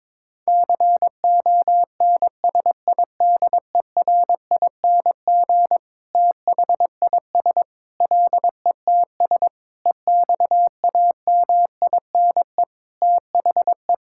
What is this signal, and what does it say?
Morse code, heard as a keyed tone, about 22 wpm, 700 Hz. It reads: CONSIDERING THIS LETS EXAMINE THE
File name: Sentences with 9-Word Phrases - 22wpm